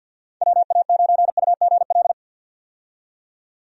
Morse code, heard as a keyed tone, about 51 wpm, 700 Hz. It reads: WA9UGL
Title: USA Call Signs - 6 Characters - 50wpm